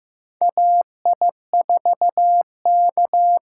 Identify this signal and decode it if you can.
Morse code, heard as a keyed tone, about 15 wpm, 700 Hz. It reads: AI4K
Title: USA Call Signs - 5 Characters - 8x Character Spacing - 15wpm